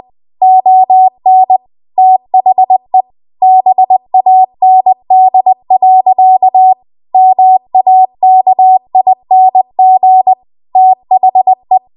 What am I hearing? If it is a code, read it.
Morse code, heard as a keyed tone, about 20 wpm, 750 Hz. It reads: ON THE BAND. MAKING THE
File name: Archive / 20 WPM / 20WPM